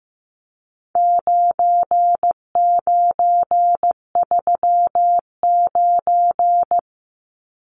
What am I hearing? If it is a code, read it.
Morse code, heard as a keyed tone, about 15 wpm, 700 Hz. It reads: 9939
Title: Numbers - 4-Digit 15wpm